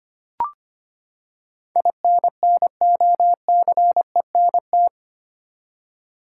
Morse code, heard as a keyed tone, about 25 wpm, 700 Hz. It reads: INNOCENT
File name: Eight Letter Words - 25wpm